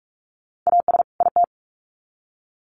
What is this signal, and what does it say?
Morse code, heard as a keyed tone, about 44 wpm, 700 Hz. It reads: AS IT
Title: Sets of 2 Words 45wpm